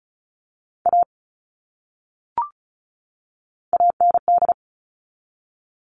Morse code, heard as a keyed tone, about 34 wpm, 700 Hz. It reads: A AND
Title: Top 500 Words 35wpm